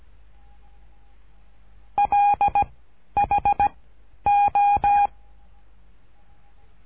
Morse code, heard as a keyed tone, about 17 wpm, 820 Hz. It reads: L H O
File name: cursos / CURSO DE TELEGRAFIA / Telegrafia - Trilha 13